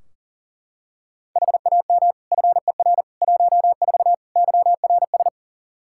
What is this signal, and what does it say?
Morse code, heard as a keyed tone, about 37 wpm, 700 Hz. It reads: HAM FER 14 YRS
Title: K1USN SST QSO Practice 40wpm